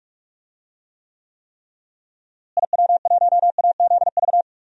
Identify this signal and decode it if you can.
Morse code, heard as a keyed tone, about 42 wpm, 700 Hz. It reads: IW1AZV
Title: DXCC-100 Call Signs - 6 Characters - Repeated 3x - 45wpm